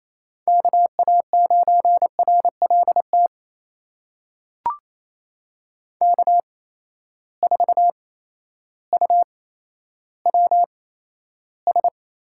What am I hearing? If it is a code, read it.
Morse code, heard as a keyed tone, about 28 wpm, spaced as for 13 wpm, 700 Hz. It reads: KA9RLT K4UWS
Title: USA Call Signs - 8x Character Spacing - 28wpm